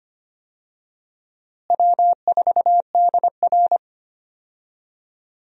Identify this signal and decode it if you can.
Morse code, heard as a keyed tone, about 25 wpm, 700 Hz. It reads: W4DR
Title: USA Call Signs - 4 Characters - Repeated 2x - 25wpm